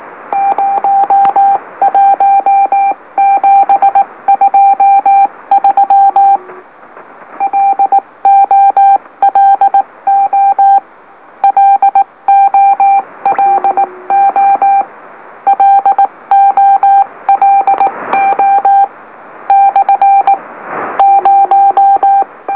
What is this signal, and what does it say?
Morse code, heard as a keyed tone, about 18 wpm, 790 Hz. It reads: 01723 LOLO LOLO LOLO / 0
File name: M04- -2345utc-LOLO